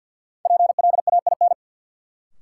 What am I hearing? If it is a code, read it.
Morse code, heard as a keyed tone, about 52 wpm, 700 Hz. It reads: PLAIN
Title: Sentences from Top 400 Words - New Word Drill - 50wpm